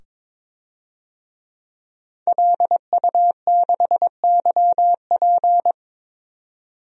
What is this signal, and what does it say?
Morse code, heard as a keyed tone, about 22 wpm, 700 Hz. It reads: LU6YP
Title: DXCC-100 Call Signs - 5 Characters - 22wpm